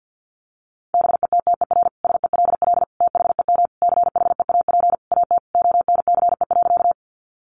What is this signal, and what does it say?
Morse code, heard as a keyed tone, about 50 wpm, 700 Hz. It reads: BETTER SELL THEM CHEAP AT ONCE.
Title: Sentences from Top 1200 Words 50wpm